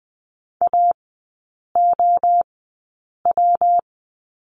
Morse code, heard as a keyed tone, about 20 wpm, 700 Hz. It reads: A O W
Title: 172 Lesson 28e - Number 9 - Three Characters 20wpm